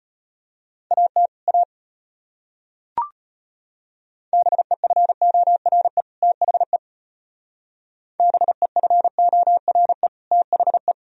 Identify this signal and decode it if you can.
Morse code, heard as a keyed tone, about 32 wpm, 700 Hz. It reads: AT A BEFORE THE BEFORE THE